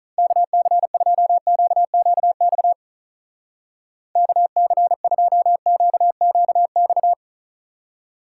Morse code, heard as a keyed tone, about 43 wpm, 700 Hz. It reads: KC2QQX KC2QQX
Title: Speed Racing - USA 2x3 Call Signs 35wpm